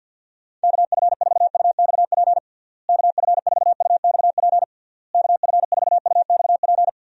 Morse code, heard as a keyed tone, about 52 wpm, 700 Hz. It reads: KF4UXP KF4UXP KF4UXP
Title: USA Call Signs - 6 Characters - Repeated 3x - 50wpm